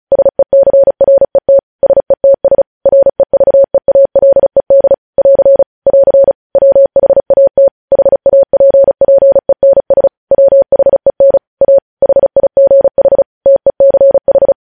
Morse code, heard as a keyed tone, about 36 wpm, 550 Hz. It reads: SECRET SETS REVEALED <AR> <AR> WHAT HAPPENS WHEN A HIGH TECH
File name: podcast / News headlines-1x-35x35